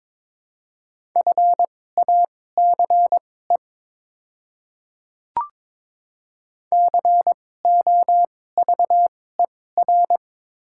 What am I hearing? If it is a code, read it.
Morse code, heard as a keyed tone, about 22 wpm, spaced as for 16 wpm, 700 Hz. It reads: FACE COVER
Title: 2x Character Spacing - Top 300 Words - 22wpm